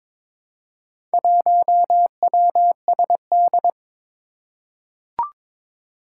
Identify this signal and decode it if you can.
Morse code, heard as a keyed tone, about 22 wpm, 700 Hz. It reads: 1WSD
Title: Four Challenging Letter-Number Combinations 22wpm